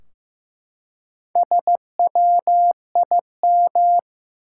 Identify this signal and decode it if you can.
Morse code, heard as a keyed tone, about 15 wpm, 700 Hz. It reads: SWIM
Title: Four Letter Words - 15wpm